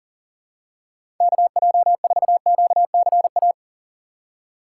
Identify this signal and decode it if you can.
Morse code, heard as a keyed tone, about 39 wpm, 690 Hz. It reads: KJ4QCA
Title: USA Call Signs - 6 Characters - 2x Character Spacing - 40wpm